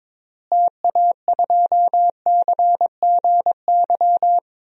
Morse code, heard as a keyed tone, about 22 wpm, 700 Hz. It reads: TA2CGY
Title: DXCC-100 Call Signs - 6 Characters - Repeated 2x - 22wpm